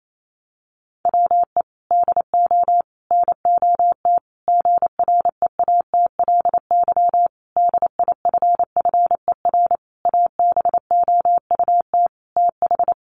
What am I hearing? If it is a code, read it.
Morse code, heard as a keyed tone, about 28 wpm, 700 Hz. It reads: WE DO NOT GREATLY DIFFER ABOUT TH
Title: Sentences from Top 1400 Words 28wpm